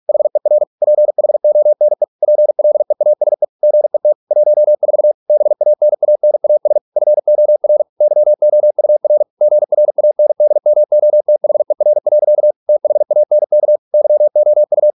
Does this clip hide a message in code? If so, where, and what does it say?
Yes: Morse code from 0.1 to 14.9 s, heard as a keyed tone, about 46 wpm, 600 Hz. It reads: HER PHONE PLEASE GET 14 BANANAS FOR YOUR GRANDMOTHER. THANK YOU